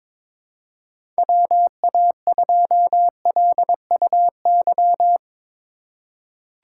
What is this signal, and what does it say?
Morse code, heard as a keyed tone, about 22 wpm, 700 Hz. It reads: WA2LUY
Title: USA Call Signs - 6 Characters - 3x Character Spacing - 22wpm